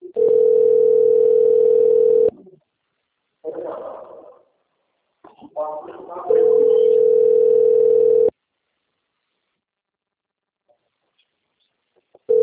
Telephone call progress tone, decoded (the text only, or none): ringback tone